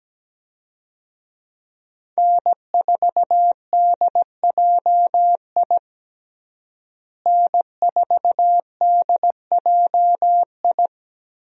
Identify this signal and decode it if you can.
Morse code, heard as a keyed tone, about 17 wpm, 700 Hz. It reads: N4DJI N4DJI
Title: USA Call Signs - 5 Characters - Repeated 3x - 17wpm